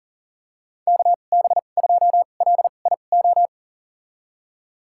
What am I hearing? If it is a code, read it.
Morse code, heard as a keyed tone, about 40 wpm, spaced as for 29 wpm, 690 Hz. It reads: KB2LIO
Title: USA Call Signs - 6 Characters - 2x Character Spacing - 40wpm